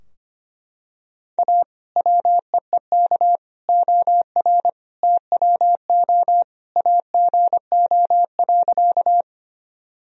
Morse code, heard as a keyed tone, about 25 wpm, 700 Hz. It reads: A WEEK OR TWO AGO.
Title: Sentences from Top 1000 Words - Review Part 3 - 25wpm